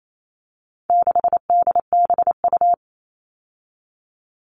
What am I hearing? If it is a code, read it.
Morse code, heard as a keyed tone, about 28 wpm, 700 Hz. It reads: 6DBU
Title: All Cut Numbers - 4-Digit 28wpm